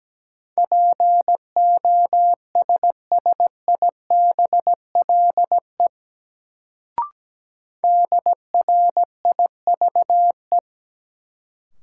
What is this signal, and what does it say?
Morse code, heard as a keyed tone, about 17 wpm, 700 Hz. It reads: POSSIBLE DRIVE
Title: Sentences from Top 500 Words - New Word Drill - 17wpm